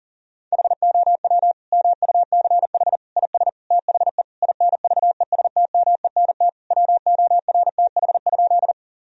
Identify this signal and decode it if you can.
Morse code, heard as a keyed tone, about 40 wpm, 700 Hz. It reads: HOW MUCH IS THE INVESTMENT WORTH?
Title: Sentences from Top 1000 Words - Review Part 4 - 40wpm